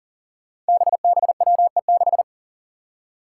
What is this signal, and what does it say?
Morse code, heard as a keyed tone, about 40 wpm, 700 Hz. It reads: BBWE6
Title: Five Challenging Letter-Number Combinations 40wpm